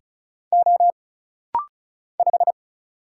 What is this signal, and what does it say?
Morse code, heard as a keyed tone, about 34 wpm, 700 Hz. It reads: O 5
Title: Single Letter-Number - Rapid-Fire 35wpm